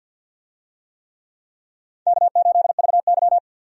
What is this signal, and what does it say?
Morse code, heard as a keyed tone, about 47 wpm, 700 Hz. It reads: K8VY
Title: USA Call Signs - 4 Characters - 50wpm